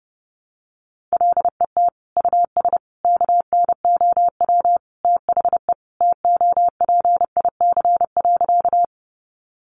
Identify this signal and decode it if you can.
Morse code, heard as a keyed tone, about 30 wpm, 700 Hz. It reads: LET US KNOW THE TOPIC.
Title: Sentences from Top 1200 Words 30wpm